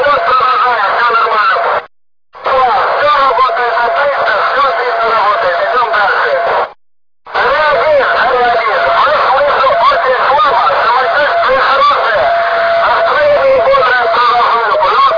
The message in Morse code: none